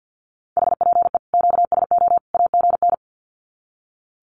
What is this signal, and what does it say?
Morse code, heard as a keyed tone, about 50 wpm, 700 Hz. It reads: HPE QSO AGN
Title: Common QSO Phrases 50wpm